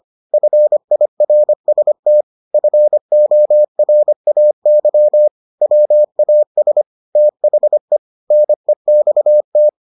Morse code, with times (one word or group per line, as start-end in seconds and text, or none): 0.3-2.2 FIRST
2.5-5.3 FORAY
5.6-6.8 WAS
7.2-8.0 THE
8.3-9.7 NEXT